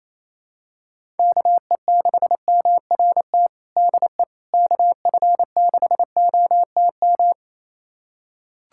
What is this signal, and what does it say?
Morse code, heard as a keyed tone, about 27 wpm, 700 Hz. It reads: KE6MRT DE KF6OTM
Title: USA Callsign DE Callsign - 6 Characters - 28wpm